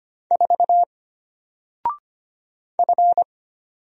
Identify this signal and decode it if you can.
Morse code, heard as a keyed tone, about 25 wpm, 700 Hz. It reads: 4 F
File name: Single-Letter-Number-25wpm